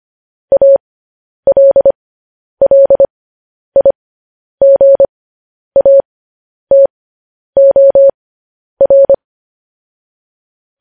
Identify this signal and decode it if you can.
Morse code, heard as a keyed tone, about 25 wpm, spaced as for 10 wpm, 550 Hz. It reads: ALLIGATOR
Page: news-headlines-in-morse-code-at-10-wpm